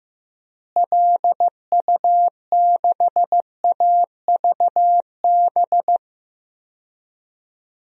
Morse code, heard as a keyed tone, about 15 wpm, 700 Hz. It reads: LU6AVB
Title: DXCC-100 Call Signs - 6 Characters - Repeated 3x - 15wpm